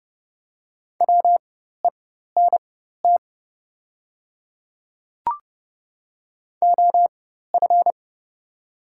Morse code, heard as a keyed tone, about 30 wpm, spaced as for 14 wpm, 700 Hz. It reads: WENT OF
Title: 4x Character Spacing - Top 200 Words - 30wpm